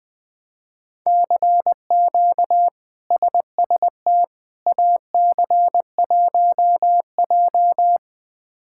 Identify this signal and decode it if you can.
Morse code, heard as a keyed tone, about 21 wpm, 700 Hz. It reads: CQ SST AC1J
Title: K1USN SST QSO Practice - 20wpm